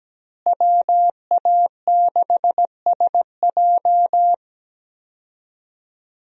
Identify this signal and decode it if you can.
Morse code, heard as a keyed tone, about 17 wpm, 700 Hz. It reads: WA6SJ
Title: USA Call Signs - 5 Characters - Repeated 2x - 17wpm